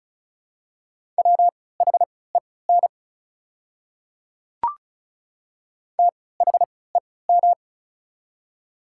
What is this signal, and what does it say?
Morse code, heard as a keyed tone, about 34 wpm, spaced as for 20 wpm, 700 Hz. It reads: WHEN THEM
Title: 3x Character Spacing - Top 100 Words - 35wpm